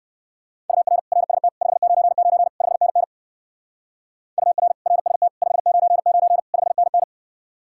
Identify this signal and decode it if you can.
Morse code, heard as a keyed tone, about 59 wpm, 700 Hz. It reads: UR RST 599 5NN UR RST 599 5NN